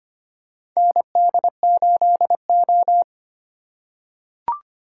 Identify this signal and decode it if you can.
Morse code, heard as a keyed tone, about 25 wpm, 700 Hz. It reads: ND8O
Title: USA Call Signs - 4 Characters - 3x Character Spacing - 25wpm